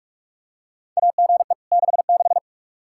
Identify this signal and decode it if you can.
Morse code, heard as a keyed tone, about 46 wpm, 700 Hz. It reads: AGE 66
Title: Practice QSO - Exchange 3 - More Info - 45wpm